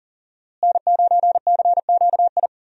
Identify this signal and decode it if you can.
Morse code, heard as a keyed tone, about 40 wpm, 700 Hz. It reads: N9CQI
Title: USA Call Signs - 5 Characters - Repeated 2x - 40wpm